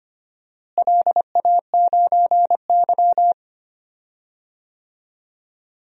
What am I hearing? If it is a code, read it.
Morse code, heard as a keyed tone, about 25 wpm, 700 Hz. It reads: LA9Y
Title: DXCC-100 Call Signs - 4 Characters - Repeated 2x - 25wpm